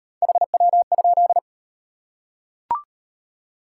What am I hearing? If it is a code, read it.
Morse code, heard as a keyed tone, about 36 wpm, 690 Hz. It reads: HW?